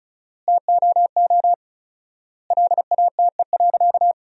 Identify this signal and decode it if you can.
Morse code, heard as a keyed tone, about 34 wpm, 700 Hz. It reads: TOO LATE.
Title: 4x Word-Spacing - Sentences from Top 300 Words - 35wpm